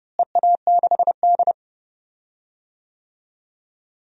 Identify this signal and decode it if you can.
Morse code, heard as a keyed tone, about 30 wpm, 700 Hz. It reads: EA6D
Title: DXCC-100 Call Signs - 4 Characters - Repeated 3x - 30wpm